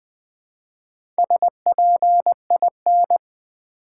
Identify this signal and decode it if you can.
Morse code, heard as a keyed tone, about 20 wpm, 700 Hz. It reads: SPIN
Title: Build A Word - 1 Letter At A Time - 3 to 5-Letter Words - 20wpm